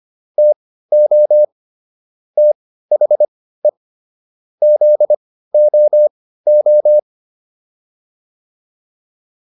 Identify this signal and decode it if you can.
Morse code, heard as a keyed tone, about 25 wpm, spaced as for 15 wpm, 600 Hz. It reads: TO THE ZOO